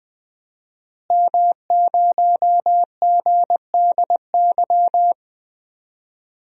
Morse code, heard as a keyed tone, about 20 wpm, 700 Hz. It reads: M0GDY
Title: DXCC-100 Call Signs - 5 Characters - Repeated 3x - 20wpm